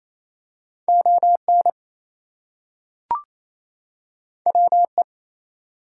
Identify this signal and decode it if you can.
Morse code, heard as a keyed tone, about 28 wpm, 700 Hz. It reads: ON WE
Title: Two Letter Words - 28wpm